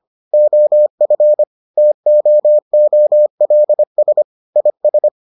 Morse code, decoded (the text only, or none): OF TOOLS IS